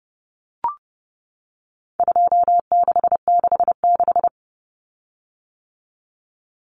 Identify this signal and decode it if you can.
Morse code, heard as a keyed tone, about 30 wpm, 700 Hz. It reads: 2666
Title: Common Cut Numbers - 4-Digit 30wpm